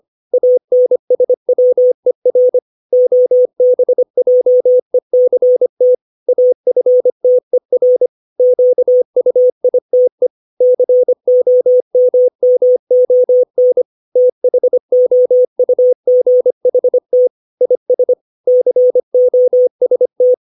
Morse code, heard as a keyed tone, about 25 wpm, 500 Hz. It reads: ANSWER OBJECT AFTER QUITE COMMON THOUGHT IS COST
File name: 632 words / 632 Words 25WPM